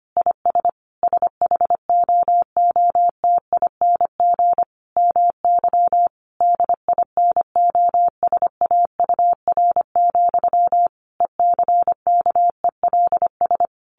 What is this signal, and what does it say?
Morse code, heard as a keyed tone, about 25 wpm, 700 Hz. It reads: IS SHOOTING MY DINOSAUR, ECKELS
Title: A Sound Of Thunder 25wpm